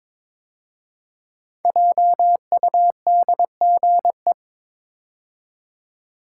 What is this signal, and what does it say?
Morse code, heard as a keyed tone, about 22 wpm, 700 Hz. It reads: JUDGE
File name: 5 Letter Words - First 3 Letters Given - 22wpm